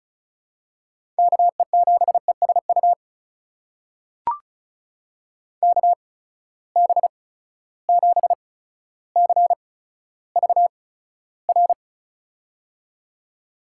Morse code, heard as a keyed tone, about 36 wpm, spaced as for 15 wpm, 700 Hz. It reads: KE7ESU KB7CVR
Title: USA Call Signs - 6 Characters - 8x Character Spacing - 35wpm